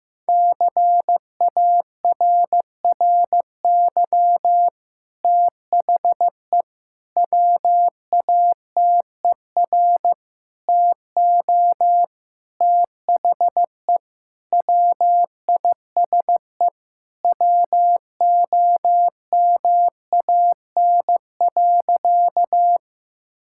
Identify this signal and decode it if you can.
Morse code, heard as a keyed tone, about 16 wpm, 700 Hz. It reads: CARRY THE WATER TO THE WISE WOMAN.
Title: Sentences from Top 1600 Words 15wpm